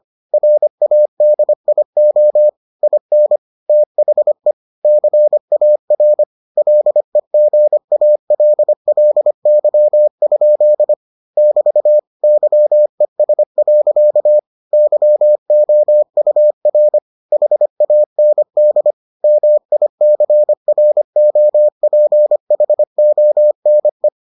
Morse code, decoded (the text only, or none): RADIO IN THE CAR LEGALLY? <BT> YES. YOUR HAND MICROPHONE